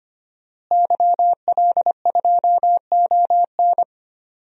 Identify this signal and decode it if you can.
Morse code, heard as a keyed tone, about 25 wpm, 700 Hz. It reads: YL2ON
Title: DXCC-100 Call Signs - 5 Characters - Repeated 2x - 25wpm